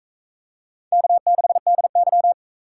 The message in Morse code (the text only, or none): K6DY